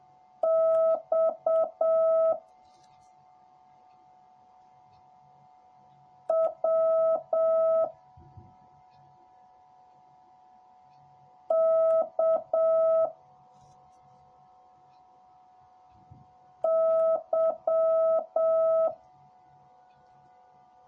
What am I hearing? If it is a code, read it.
Morse code, heard as a keyed tone, about 7 wpm, 650 Hz. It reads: X W K Y